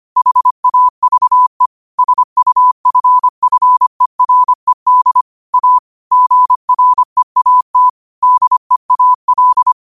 Morse code, heard as a keyed tone, about 25 wpm, 1000 Hz. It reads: HAVE SUFFERED A GREAT DEAL
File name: qotdfiles / 25wpm